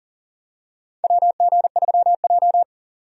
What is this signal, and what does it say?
Morse code, heard as a keyed tone, about 40 wpm, 700 Hz. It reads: WG3J
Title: USA Call Signs - 4 Characters - 40wpm